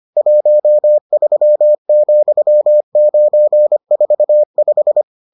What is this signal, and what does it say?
Morse code, heard as a keyed tone, about 25 wpm, 600 Hz. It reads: 13,945